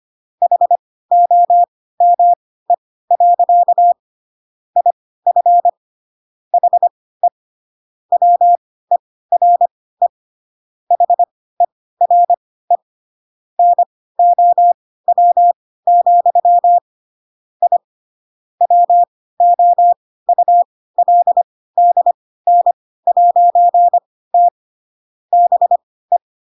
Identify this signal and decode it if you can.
Morse code, heard as a keyed tone, about 25 wpm, spaced as for 16 wpm, 700 Hz. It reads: HOME. IF HE WERE HERE NOW, I WOULDN'T BE